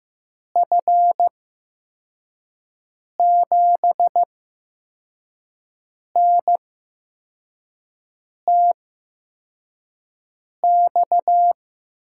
Morse code, heard as a keyed tone, about 15 wpm, spaced as for 4 wpm, 700 Hz. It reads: F7NTX
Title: USA Call Signs - 6 Characters - 8x Character Spacing - 15wpm